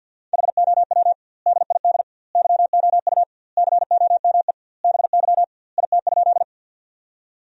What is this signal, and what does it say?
Morse code, heard as a keyed tone, about 50 wpm, 700 Hz. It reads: HOW DID YOU COME BY IT?
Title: 2x Word-Spacing - Sentences from Top 100 Words - 50wpm